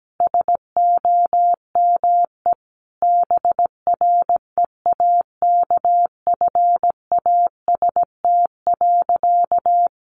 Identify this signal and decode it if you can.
Morse code, heard as a keyed tone, about 17 wpm, 700 Hz. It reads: SOME BREAKFAST.
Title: Sentences from Top 1500 Words 17wpm